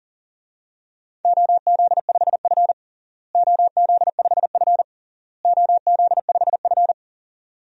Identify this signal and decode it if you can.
Morse code, heard as a keyed tone, about 40 wpm, 700 Hz. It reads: OZ5F OZ5F OZ5F
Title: DXCC-100 Call Signs - 4 Characters - Repeated 3x - 40wpm